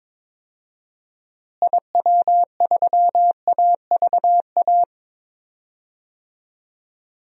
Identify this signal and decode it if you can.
Morse code, heard as a keyed tone, about 21 wpm, 700 Hz. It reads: IW3AVA
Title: DXCC-100 Call Signs - 6 Characters - Repeated 2x - 22wpm